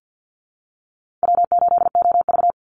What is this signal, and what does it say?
Morse code, heard as a keyed tone, about 50 wpm, 700 Hz. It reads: W8OV
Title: Easy Call Signs - 50wpm